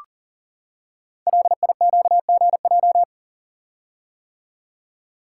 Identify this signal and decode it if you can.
Morse code, heard as a keyed tone, about 39 wpm, 700 Hz. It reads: LIQGJ
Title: Code Groups - 5 Letters - 40wpm